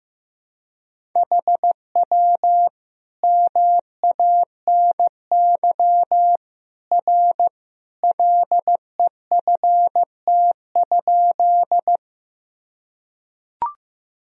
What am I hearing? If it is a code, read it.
Morse code, heard as a keyed tone, about 16 wpm, 700 Hz. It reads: HW MANY R LEFT?
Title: Sentences from Top 300 Words and Abbreviations 15wpm